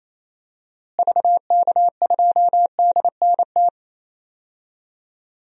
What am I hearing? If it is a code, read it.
Morse code, heard as a keyed tone, about 28 wpm, 700 Hz. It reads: VK2DNT
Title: DXCC-100 Call Signs - 6 Characters - Repeated 3x - 28wpm